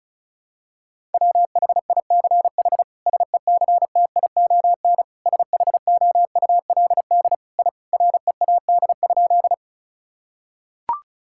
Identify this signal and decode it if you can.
Morse code, heard as a keyed tone, about 36 wpm, 700 Hz. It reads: WHICH SECTION SHOULD I READ?
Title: Sentences from Top 1000 Words - Review Part 5 - 35wpm